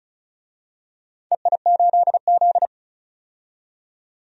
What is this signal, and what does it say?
Morse code, heard as a keyed tone, about 36 wpm, 700 Hz. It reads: EI8Z
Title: DXCC-100 Call Signs - 4 Characters - 35wpm